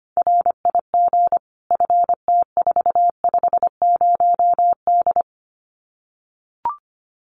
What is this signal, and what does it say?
Morse code, heard as a keyed tone, about 25 wpm, 700 Hz. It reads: RIG FT450D
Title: Common QSO Phrases 25wpm